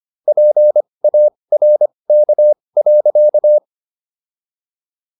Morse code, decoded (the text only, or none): PARK.